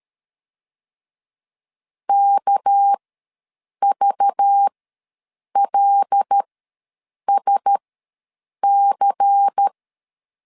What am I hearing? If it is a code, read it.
Morse code, heard as a keyed tone, about 13 wpm, spaced as for 7 wpm, 800 Hz. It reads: KVLSC
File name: LEZIONI / 6 PKX0 / 6 PK ws 12wpm 3116